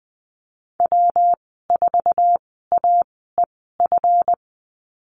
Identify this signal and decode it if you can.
Morse code, heard as a keyed tone, about 20 wpm, spaced as for 15 wpm, 700 Hz. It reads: W4AEF
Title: 161 Lesson 26g - Letter g - Call Signs 20wpm